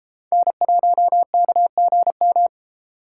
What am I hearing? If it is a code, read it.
Morse code, heard as a keyed tone, about 33 wpm, 700 Hz. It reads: N1KGM